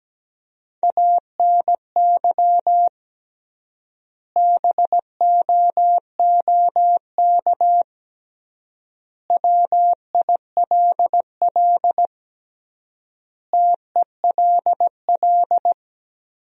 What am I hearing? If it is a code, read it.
Morse code, heard as a keyed tone, about 17 wpm, 700 Hz. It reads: ANY BOOK WILL TELL
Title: 3x Word-Spacing - Sentences from Top 300 Words - 17wpm